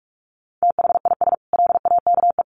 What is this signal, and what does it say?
Morse code, heard as a keyed tone, about 44 wpm, 700 Hz. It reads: THIS LAKE
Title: Sentences from Top 1100 Words 45wpm